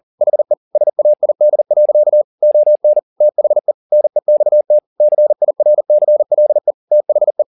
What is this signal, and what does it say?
Morse code, heard as a keyed tone, about 40 wpm, 600 Hz. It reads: HE SAID. ON THE NEXT CIRCLE THE
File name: the old man and the sea 2 40wpm